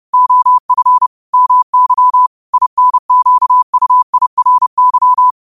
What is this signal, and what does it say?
Morse code, heard as a keyed tone, about 30 wpm, 1000 Hz. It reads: OF MY INQUIRY